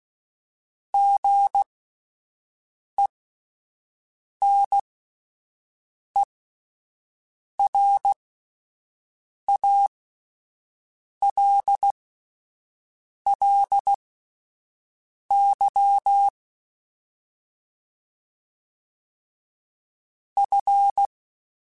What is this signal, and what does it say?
Morse code, heard as a keyed tone, about 16 wpm, spaced as for 5 wpm, 800 Hz. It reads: GENERALLY F
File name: Spaced Characters / Spaced Characters-mp3s / Weather 5 22 sp @10